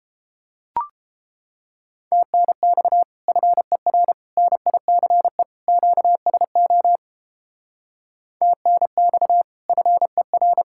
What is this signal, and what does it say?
Morse code, heard as a keyed tone, about 31 wpm, 700 Hz. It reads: TNX FER NICE QSO TNX FER